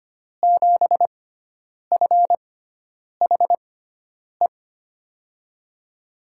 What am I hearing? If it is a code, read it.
Morse code, heard as a keyed tone, about 25 wpm, spaced as for 9 wpm, 700 Hz. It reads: 7FHE
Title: USA Call Signs - 6 Characters - 6x Character Spacing - 25wpm